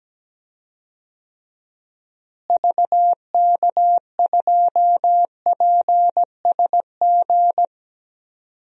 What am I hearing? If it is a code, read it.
Morse code, heard as a keyed tone, about 17 wpm, 700 Hz. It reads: VK2PSG
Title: DXCC-100 Call Signs - 6 Characters - Repeated 2x - 17wpm